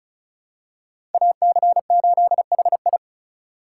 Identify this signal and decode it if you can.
Morse code, heard as a keyed tone, about 36 wpm, 700 Hz. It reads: AC8HI
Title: USA Call Signs - 2x Character Spacing - 35wpm